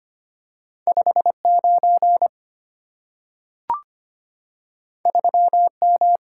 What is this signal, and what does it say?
Morse code, heard as a keyed tone, about 25 wpm, 700 Hz. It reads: 59 3M
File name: Two Letter-Number Combinations - 25wpm